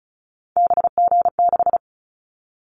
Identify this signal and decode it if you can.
Morse code, heard as a keyed tone, about 36 wpm, 700 Hz. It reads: BG6
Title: Callsign Prefixes for Top 100 DXCC - 35wpm